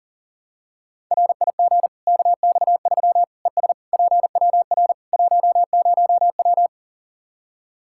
Morse code, heard as a keyed tone, about 40 wpm, 700 Hz. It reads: RIG KX3 ES PWR 10W
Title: Practice QSO - Exchange 2 - Info - 40wpm